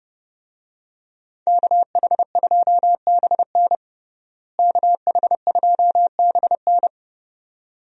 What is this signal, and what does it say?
Morse code, heard as a keyed tone, about 30 wpm, 700 Hz. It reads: KH2BN KH2BN
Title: USA Call Signs - 5 Characters - Repeated 2x - 30wpm